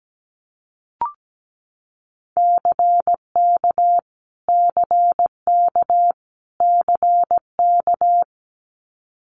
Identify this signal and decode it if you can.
Morse code, heard as a keyed tone, about 17 wpm, 700 Hz. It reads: CK CK CK
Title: Multi-letter Phonemes Repeated 3x 17wpm